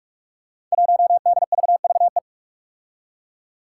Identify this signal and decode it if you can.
Morse code, heard as a keyed tone, about 46 wpm, 700 Hz. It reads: 1DVVE
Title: Five Challenging Letter-Number Combinations 45wpm